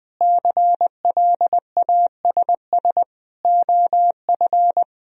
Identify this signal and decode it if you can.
Morse code, heard as a keyed tone, about 21 wpm, 700 Hz. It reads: CLASS OF
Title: Sentences from Top 1000 Words - Review Part 2 - 20wpm